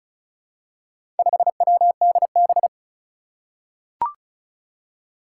Morse code, heard as a keyed tone, about 36 wpm, 700 Hz. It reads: 5WDB
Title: Four Challenging Letter-Number Combinations 35wpm